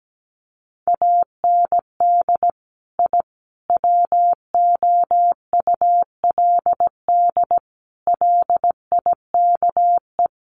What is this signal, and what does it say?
Morse code, heard as a keyed tone, about 17 wpm, 700 Hz. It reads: AND I WOULD LIKE
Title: Sets of 5 Words Encore 17wpm